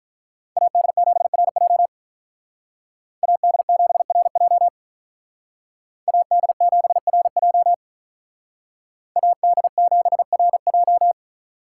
Morse code, heard as a keyed tone, about 45 wpm, 700 Hz. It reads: AD7RJ AD7RJ AD7RJ AD7RJ